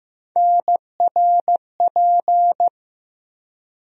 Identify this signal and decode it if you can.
Morse code, heard as a keyed tone, about 15 wpm, 700 Hz. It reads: NRP
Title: USA Call Sign Postfixes - 3 Letters 15wpm